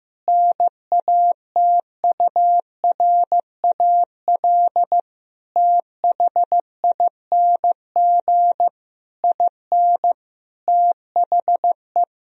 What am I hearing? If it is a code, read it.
Morse code, heard as a keyed tone, about 15 wpm, 700 Hz. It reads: NATURAL THING IN THE
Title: Sentences from Top 1000 Words - Review Part 4 - 15wpm